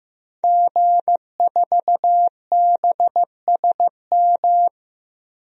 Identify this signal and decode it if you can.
Morse code, heard as a keyed tone, about 15 wpm, 700 Hz. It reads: G4BSM
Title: DXCC-100 Call Signs - 5 Characters - Repeated 3x - 15wpm